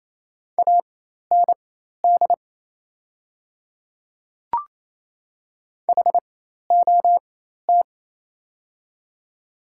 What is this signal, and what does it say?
Morse code, heard as a keyed tone, about 28 wpm, spaced as for 13 wpm, 700 Hz. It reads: AND HOT
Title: 4x Character Spacing - Top 500 Words - 28wpm